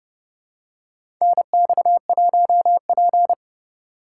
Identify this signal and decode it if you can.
Morse code, heard as a keyed tone, about 30 wpm, 700 Hz. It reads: NX1P
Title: USA Call Signs - 4 Characters - 2x Character Spacing - 30wpm